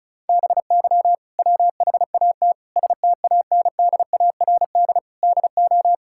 Morse code, heard as a keyed tone, about 36 wpm, 700 Hz. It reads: BY WHAT STANDARD DO
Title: Sentences from Top 1000 Words - Review Part 6 - 35wpm